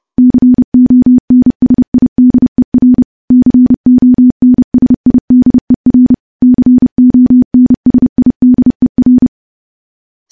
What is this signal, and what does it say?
Morse code, heard as a keyed tone, about 32 wpm, 250 Hz. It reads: CONSIDER CONSIDER CONSIDER
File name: morse / morse-training-30WPM